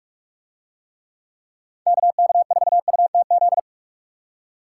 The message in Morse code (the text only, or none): KK4UTZ